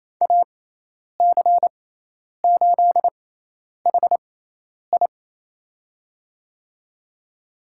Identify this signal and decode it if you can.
Morse code, heard as a keyed tone, about 27 wpm, spaced as for 10 wpm, 700 Hz. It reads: AC8HI
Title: USA Call Signs - 5 Characters - 6x Character Spacing - 28wpm